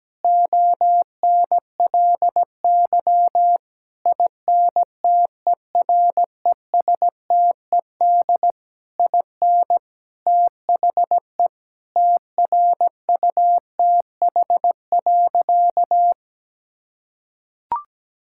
Morse code, heard as a keyed tone, about 17 wpm, 700 Hz. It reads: ONLY INTERESTED IN THE TRUTH.
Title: Sentences from Top 1000 Words - Review Part 6 - 17wpm